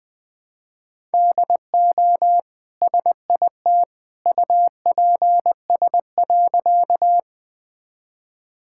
Morse code, heard as a keyed tone, about 21 wpm, 700 Hz. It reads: DO SIT UPS.
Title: Easy Sentences with Hints - 20wpm